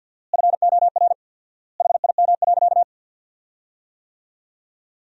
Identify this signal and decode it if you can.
Morse code, heard as a keyed tone, about 51 wpm, 700 Hz. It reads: FOR HIM.
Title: 4x Word-Spacing - Sentences from Top 500 Words - 50wpm